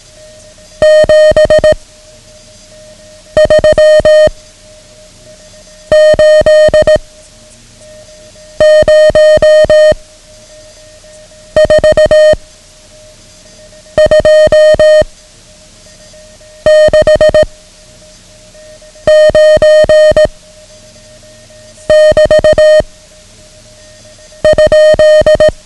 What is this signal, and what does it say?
Morse code, heard as a keyed tone, about 18 wpm, 620 Hz. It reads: 7 3 8 0 4 2 6 9 <BT> ?
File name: ham radio / CodeQuick / Code Quick 09